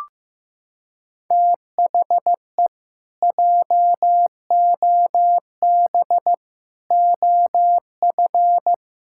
Morse code, heard as a keyed tone, about 15 wpm, 700 Hz. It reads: THE JOB OF
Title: Pangram Challenge v2 - 15wpm